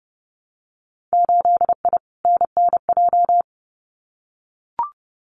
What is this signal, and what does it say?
Morse code, heard as a keyed tone, about 30 wpm, 700 Hz. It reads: 8I NNJ
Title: Winter Field Day Exchanges Encore 30wpm